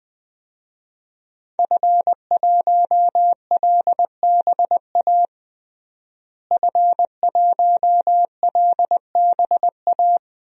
Morse code, heard as a keyed tone, about 20 wpm, 700 Hz. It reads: F1LBA F1LBA
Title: DXCC-100 Call Signs - 5 Characters - Repeated 2x - 20wpm